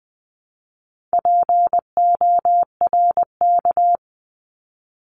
Morse code, heard as a keyed tone, about 20 wpm, 700 Hz. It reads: PORK
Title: 260 Lesson 42e - Words 20-20wpm